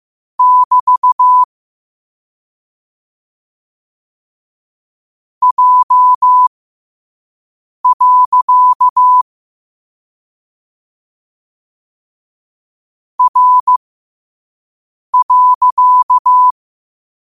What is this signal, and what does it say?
Morse code, heard as a keyed tone, about 15 wpm, spaced as for 5 wpm, 1000 Hz. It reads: <BT> J. R.
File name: qotdfiles / 5wpm